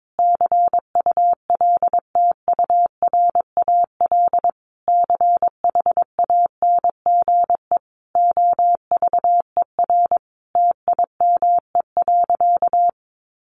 Morse code, heard as a keyed tone, about 22 wpm, 700 Hz. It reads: CULTURAL CHANGE OVER TIME.
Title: Sentences from Top 1500 Words 22wpm